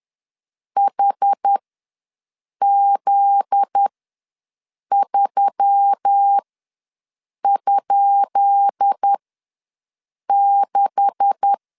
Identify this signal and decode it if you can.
Morse code, heard as a keyed tone, about 10 wpm, spaced as for 6 wpm, 800 Hz. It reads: HZ3?6
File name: LEZIONI / 10 SIGNS / 10B INT ws 10wpm 4134